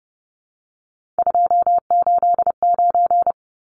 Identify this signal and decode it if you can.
Morse code, heard as a keyed tone, about 30 wpm, 700 Hz. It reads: 289
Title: Numbers - 3-Digit 30wpm